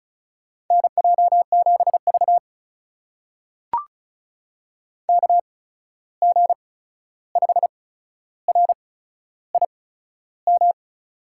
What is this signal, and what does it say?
Morse code, heard as a keyed tone, about 36 wpm, spaced as for 13 wpm, 700 Hz. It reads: NJ7V KG5RIM
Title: USA Call Signs - 8x Character Spacing - 35wpm